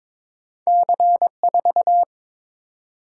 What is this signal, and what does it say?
Morse code, heard as a keyed tone, about 22 wpm, 700 Hz. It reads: C4
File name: practice / 22wpm / Two Letter-Number Combinations - 22wpm